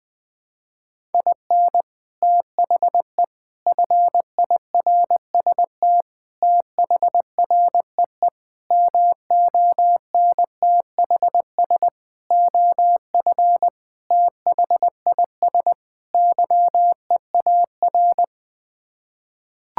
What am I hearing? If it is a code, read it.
Morse code, heard as a keyed tone, about 21 wpm, 700 Hz. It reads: IN THE FIRST THREE MONTHS OF THIS YEAR
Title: Sets of 8 Words - 20wpm